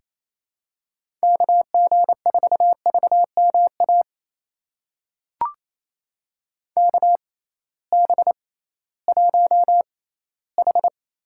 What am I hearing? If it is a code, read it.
Morse code, heard as a keyed tone, about 28 wpm, spaced as for 16 wpm, 700 Hz. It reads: KG4VMA KB1H
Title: USA Call Signs - 6 Characters - 6x Character Spacing - 28wpm